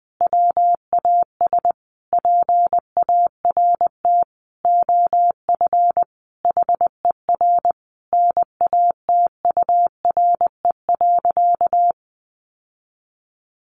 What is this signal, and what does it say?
Morse code, heard as a keyed tone, about 20 wpm, 700 Hz. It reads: WAS PART OF HER NATURE.
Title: Sentences from Top 1300 Words 20wpm